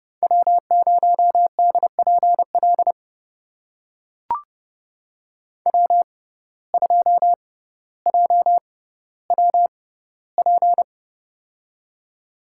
Morse code, heard as a keyed tone, about 32 wpm, spaced as for 16 wpm, 700 Hz. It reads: W0DPL W2JWP
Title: USA Call Signs - 5 Characters - 6x Character Spacing - 30wpm